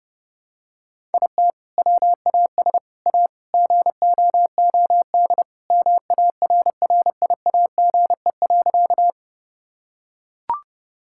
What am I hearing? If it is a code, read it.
Morse code, heard as a keyed tone, about 32 wpm, 700 Hz. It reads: IT WAS A GOOD MARRIAGE.
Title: Sentences from Top 1000 Words - Review Part 5 - 30wpm